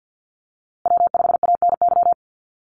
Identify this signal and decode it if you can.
Morse code, heard as a keyed tone, about 50 wpm, 700 Hz. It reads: W5ANY
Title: USA Call Signs Repeated 3x 50wpm